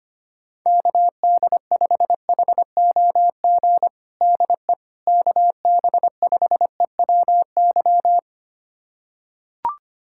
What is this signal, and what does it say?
Morse code, heard as a keyed tone, about 25 wpm, 700 Hz. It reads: KD5HOG DE KB5EWY
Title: USA Callsign DE Callsign - 6 Characters - 25wpm